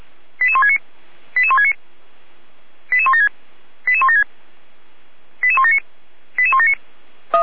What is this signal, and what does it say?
Five-tone selective call (ZVEI), heard as quick runs of five tones, tones of about 70 ms. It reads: 88169 88169 88176 88176 88179 88179